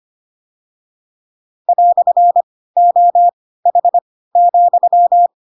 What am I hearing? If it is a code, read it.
Morse code, heard as a keyed tone, about 25 wpm, spaced as for 16 wpm, 700 Hz. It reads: "OH,
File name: Pinocchio / Pinocchio ch1 10 25 ik1hge